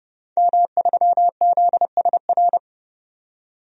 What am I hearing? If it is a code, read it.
Morse code, heard as a keyed tone, about 30 wpm, 700 Hz. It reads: M3ZSR
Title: DXCC-100 Call Signs - 5 Characters - 30wpm